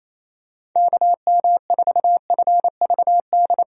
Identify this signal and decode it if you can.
Morse code, heard as a keyed tone, about 28 wpm, 700 Hz. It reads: KM4FVD
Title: USA Call Signs - 6 Characters - 28wpm